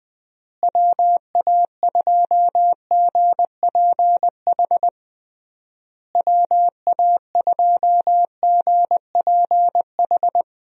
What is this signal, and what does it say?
Morse code, heard as a keyed tone, about 20 wpm, 700 Hz. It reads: WA2GPH WA2GPH
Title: USA Call Signs - 6 Characters - Repeated 3x - 20wpm